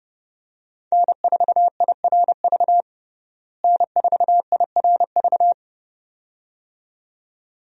Morse code, heard as a keyed tone, about 30 wpm, 700 Hz. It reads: N4IRV N4IRV
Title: USA Call Signs - 5 Characters - Repeated 2x - 30wpm